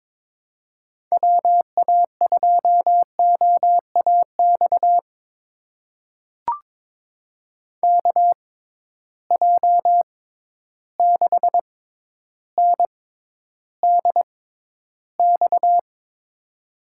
Morse code, heard as a keyed tone, about 23 wpm, spaced as for 11 wpm, 700 Hz. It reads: WA2OAX KJ6NDX
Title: USA Call Signs - 6 Characters - 6x Character Spacing - 22wpm